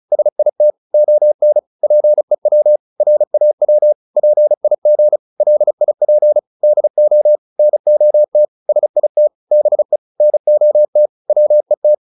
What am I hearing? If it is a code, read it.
Morse code, heard as a keyed tone, about 34 wpm, 600 Hz. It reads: SIT ON PEW RAW PIG LIP DO NOT SIT BE NOT WET